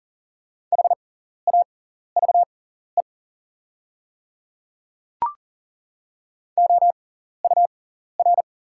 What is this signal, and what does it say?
Morse code, heard as a keyed tone, about 40 wpm, spaced as for 14 wpm, 690 Hz. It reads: HAVE OUR
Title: 6x Character Spacing - Top 200 Words - 40wpm